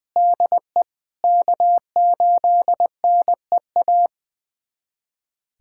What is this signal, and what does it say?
Morse code, heard as a keyed tone, about 20 wpm, 700 Hz. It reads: DE K8NEA
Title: USA Callsign DE Callsign 20wpm